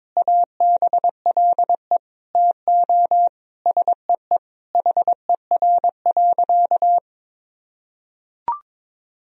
Morse code, heard as a keyed tone, about 22 wpm, 700 Hz. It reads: ABLE TO SEE HER.
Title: Sentences from Top 1000 Words - Review Part 2 - 22wpm